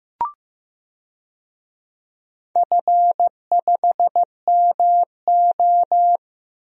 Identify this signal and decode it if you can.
Morse code, heard as a keyed tone, about 15 wpm, 700 Hz. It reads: F5MO
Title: DXCC-100 Call Signs - 4 Characters - Repeated 2x - 15wpm